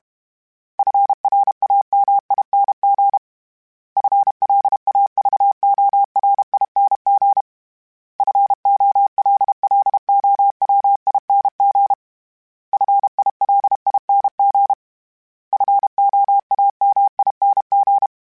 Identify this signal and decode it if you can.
Morse code, heard as a keyed tone, about 32 wpm, 800 Hz. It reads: FRAMING FLAVORING FOLLOWING FILING FOAMING
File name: Special INGs-mp3s / Special INGs-mp3s / Sngl Ing @30